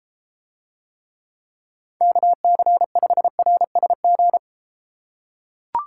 Morse code, heard as a keyed tone, about 32 wpm, 700 Hz. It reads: KC5RSG